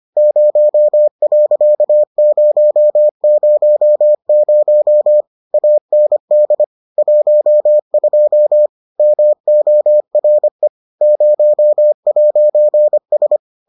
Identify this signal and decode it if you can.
Morse code, heard as a keyed tone, about 25 wpm, 600 Hz. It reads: 0.000 AND 12 MORE 0'S